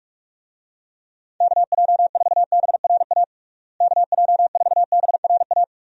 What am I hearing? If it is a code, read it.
Morse code, heard as a keyed tone, about 41 wpm, 700 Hz. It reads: KJ4BRA KJ4BRA
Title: USA Call Signs - 6 Characters - Repeated 3x - 45wpm